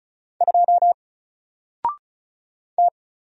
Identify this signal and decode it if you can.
Morse code, heard as a keyed tone, about 35 wpm, 700 Hz. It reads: 2 T
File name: Single-Letter-Number-35wpm